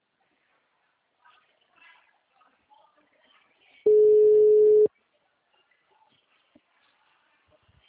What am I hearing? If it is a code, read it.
Ringback tone (Europe, 425 Hz).